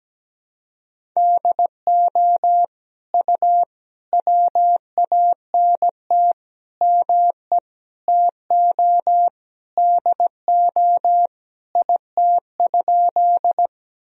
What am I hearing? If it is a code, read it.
Morse code, heard as a keyed tone, about 17 wpm, 700 Hz. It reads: DO U WANT ME TO DO IT?
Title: Sentences from Top 200 Words and Abbreviations 17wpm